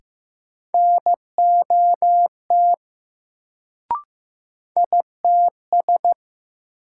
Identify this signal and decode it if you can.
Morse code, heard as a keyed tone, about 15 wpm, 700 Hz. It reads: NOT ITS
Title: Sentences from Top 600 Words - New Word Drill - 15wpm